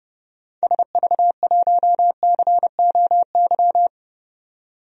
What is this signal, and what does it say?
Morse code, heard as a keyed tone, about 30 wpm, 690 Hz. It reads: SV1COY
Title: DXCC-100 Call Signs - 6 Characters - 30wpm